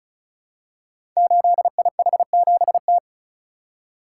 Morse code, heard as a keyed tone, about 36 wpm, 700 Hz. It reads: 8IH7T
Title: Code Groups - 5 Letter:Numbers - 35wpm